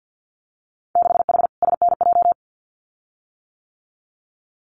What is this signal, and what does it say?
Morse code, heard as a keyed tone, about 50 wpm, 700 Hz. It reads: 6H SNJ
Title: Winter Field Day Exchanges Encore 50wpm